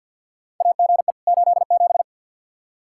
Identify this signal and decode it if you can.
Morse code, heard as a keyed tone, about 52 wpm, 700 Hz. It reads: AGE 87
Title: K1USN SST QSO Practice 50wpm